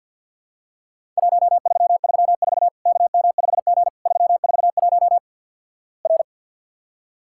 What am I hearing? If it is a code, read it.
Morse code, heard as a keyed tone, about 50 wpm, 700 Hz. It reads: 1334 KM5G 341 R
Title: Worked All Europe DX Contest - QTC Exchange with Repeats - 50wpm